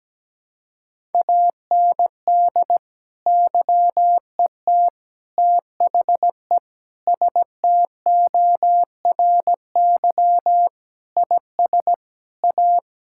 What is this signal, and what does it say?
Morse code, heard as a keyed tone, about 17 wpm, 700 Hz. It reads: AND YET THE STORY IS A
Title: Sentences from Top 1000 Words - Review Part 6 - 17wpm